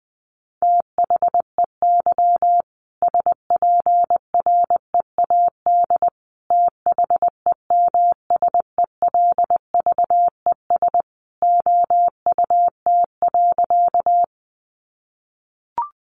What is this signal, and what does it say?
Morse code, heard as a keyed tone, about 20 wpm, 700 Hz. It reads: THEY SPREAD THEMSELVES OUT.
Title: Sentences from Top 1200 Words 20wpm